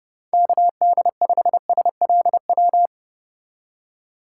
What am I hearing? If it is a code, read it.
Morse code, heard as a keyed tone, about 32 wpm, 700 Hz. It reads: KD5SLW
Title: USA Call Signs - 2x Character Spacing - 30wpm